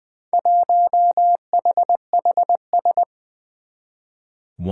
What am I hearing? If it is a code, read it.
Morse code, heard as a keyed tone, about 20 wpm, 700 Hz. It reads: 1HHS
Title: Four Challenging Letter-Number Combinations 20wpm